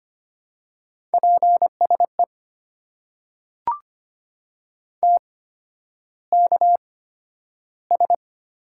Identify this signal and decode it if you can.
Morse code, heard as a keyed tone, about 25 wpm, spaced as for 11 wpm, 700 Hz. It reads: PSE TKS
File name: Top 90 QSO Elements - 8x Character Spacing - 25wpm